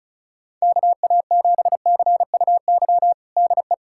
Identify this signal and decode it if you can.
Morse code, heard as a keyed tone, about 34 wpm, 700 Hz. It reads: KA7CUY DE
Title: USA Callsign DE Callsign 35wpm